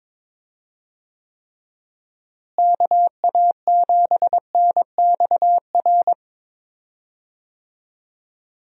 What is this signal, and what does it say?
Morse code, heard as a keyed tone, about 22 wpm, 700 Hz. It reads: KA7NXR
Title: USA Call Signs - 6 Characters - Repeated 2x - 22wpm